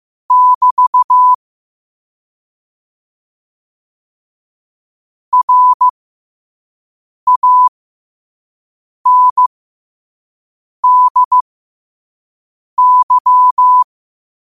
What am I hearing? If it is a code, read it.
Morse code, heard as a keyed tone, about 15 wpm, spaced as for 5 wpm, 1000 Hz. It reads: <BT> RANDY
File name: qotdfiles / 5wpm